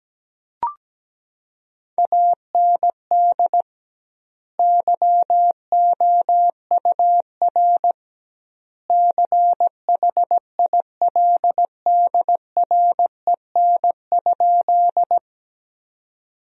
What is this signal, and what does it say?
Morse code, heard as a keyed tone, about 17 wpm, 700 Hz. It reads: AND YOUR CHILDREN?
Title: 2x Word-Spacing - Sentences from Top 300 Words - 17wpm